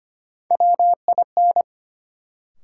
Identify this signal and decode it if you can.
Morse code, heard as a keyed tone, about 25 wpm, 700 Hz. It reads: WIN
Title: Sentences from Top 1500 Words - New Word Drill - 25wpm